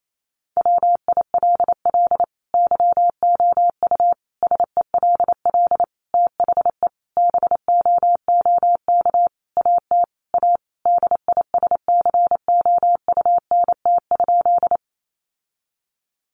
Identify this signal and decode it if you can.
Morse code, heard as a keyed tone, about 28 wpm, 700 Hz. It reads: WILL YOU SELL THE BOOK AT A DISCOUNT?
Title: Sentences from Top 1200 Words 28wpm